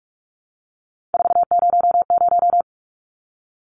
Morse code, heard as a keyed tone, about 44 wpm, 700 Hz. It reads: 400
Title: Numbers - 3-Digit 45wpm